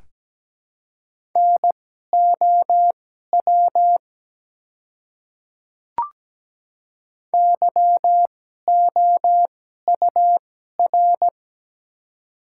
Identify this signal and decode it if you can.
Morse code, heard as a keyed tone, about 17 wpm, spaced as for 12 wpm, 700 Hz. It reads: NOW YOUR
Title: 2x Character Spacing - Top 100 Words - 17wpm